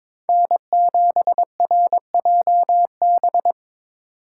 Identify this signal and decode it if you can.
Morse code, heard as a keyed tone, about 25 wpm, 700 Hz. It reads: N7RJB
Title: Speed Racing - USA 1x3 Call Signs 22wpm